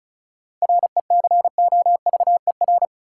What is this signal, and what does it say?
Morse code, heard as a keyed tone, about 36 wpm, 700 Hz. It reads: RECOVER
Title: Seven Letter Words - 35wpm